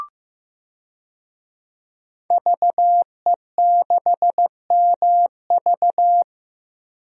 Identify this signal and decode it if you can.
Morse code, heard as a keyed tone, about 16 wpm, 700 Hz. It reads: VE6MV
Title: DXCC-100 Call Signs - 5 Characters - Repeated 3x - 15wpm